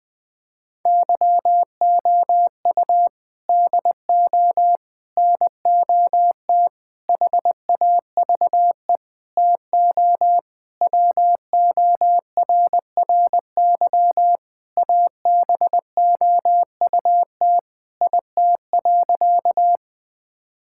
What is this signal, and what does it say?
Morse code, heard as a keyed tone, about 21 wpm, 700 Hz. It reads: YOU DO NOT HAVE TO WORRY ABOUT IT.
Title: Sentences with 5-Word Phrases - 20wpm